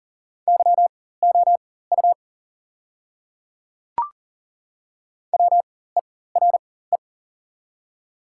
Morse code, heard as a keyed tone, about 37 wpm, spaced as for 18 wpm, 690 Hz. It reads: YOU WERE
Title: 4x Character Spacing - Top 100 Words - 40wpm